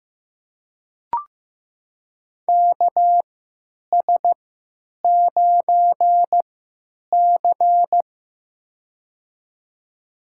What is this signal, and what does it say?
Morse code, heard as a keyed tone, about 15 wpm, spaced as for 9 wpm, 700 Hz. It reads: KS9C
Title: USA Call Signs - 4 Characters - 3x Character Spacing - 15wpm